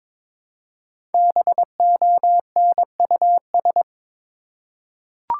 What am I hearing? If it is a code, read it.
Morse code, heard as a keyed tone, about 22 wpm, 700 Hz. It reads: BONUS
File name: 5 Letter Words - First 3 Letters Given - 22wpm